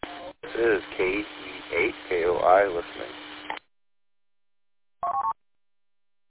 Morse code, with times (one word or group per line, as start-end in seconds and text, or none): none